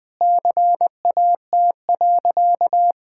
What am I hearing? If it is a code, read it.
Morse code, heard as a keyed tone, about 20 wpm, 700 Hz. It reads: CAT.
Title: Sentences with 3-Word Phrases - Spoken Beginning - 20wpm